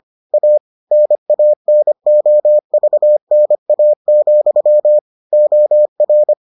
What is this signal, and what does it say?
Morse code, heard as a keyed tone, about 25 wpm, 600 Hz. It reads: A NANOVNA, OR